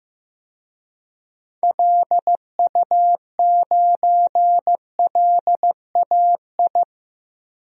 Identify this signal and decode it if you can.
Morse code, heard as a keyed tone, about 15 wpm, 700 Hz. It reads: LU9LAI
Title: DXCC-100 Call Signs - 6 Characters - 15wpm